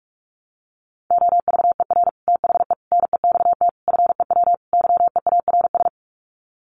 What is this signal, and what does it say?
Morse code, heard as a keyed tone, about 44 wpm, 700 Hz. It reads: OVER THE NEXT FEW YEARS
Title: Sets of 5 Words Encore 45wpm